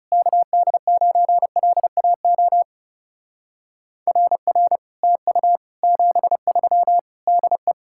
Morse code, heard as a keyed tone, about 40 wpm, 700 Hz. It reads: KD9LAO RR TU 73 DE